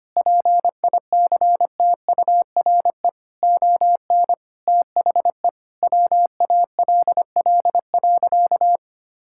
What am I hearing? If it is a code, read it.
Morse code, heard as a keyed tone, about 25 wpm, 700 Hz. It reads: PICTURE ON THE WALL.
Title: Sentences from Top 1600 Words 25wpm